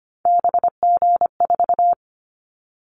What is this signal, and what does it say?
Morse code, heard as a keyed tone, about 25 wpm, 700 Hz. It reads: BG4
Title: Callsign Prefixes for Top 100 DXCC - 25wpm